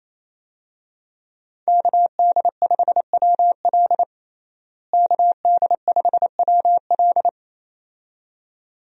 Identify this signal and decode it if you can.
Morse code, heard as a keyed tone, about 28 wpm, 700 Hz. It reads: KD5WL KD5WL
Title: USA Call Signs - 5 Characters - Repeated 2x - 28wpm